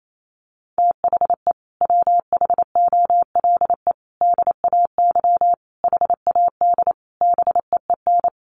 Morse code, heard as a keyed tone, about 28 wpm, 700 Hz. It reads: THE WHOLE DAY HAD BEEN
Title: Sentences from Top 500 Words 28wpm